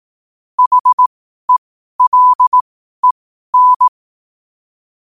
Morse code, heard as a keyed tone, about 18 wpm, spaced as for 12 wpm, 1000 Hz. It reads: HELEN
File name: qotdfiles / 12wpm